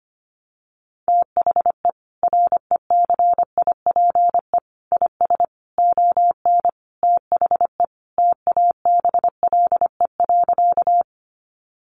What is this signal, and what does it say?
Morse code, heard as a keyed tone, about 25 wpm, 700 Hz. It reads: THE RECIPE IS ON THE TABLE.
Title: Sentences from Top 1200 Words 25wpm